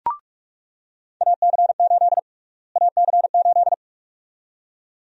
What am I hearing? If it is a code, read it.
Morse code, heard as a keyed tone, about 46 wpm, 700 Hz. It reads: AC8 AC8
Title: USA Call Sign Prefixes - Repeated 2x - 45wpm